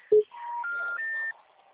Special information tone.
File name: out- -2039- - - .10844